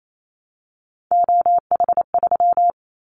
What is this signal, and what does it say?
Morse code, heard as a keyed tone, about 28 wpm, 700 Hz. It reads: OH3
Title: Callsign Prefixes for Top 25 DXCC - Repeated 3x - 28wpm